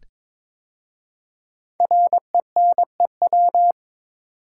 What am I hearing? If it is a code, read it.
Morse code, heard as a keyed tone, about 22 wpm, 700 Hz. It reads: RENEW
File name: 5 Letter Words - First 3 Letters Given - 22wpm